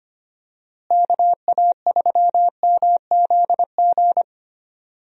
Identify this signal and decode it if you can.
Morse code, heard as a keyed tone, about 25 wpm, 700 Hz. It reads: KA3MZG
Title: USA Call Signs - 6 Characters - Repeated 3x - 25wpm